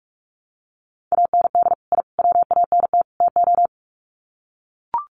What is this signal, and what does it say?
Morse code, heard as a keyed tone, about 44 wpm, 700 Hz. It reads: AND I WANT TO
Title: Sets of 4 Words 45wpm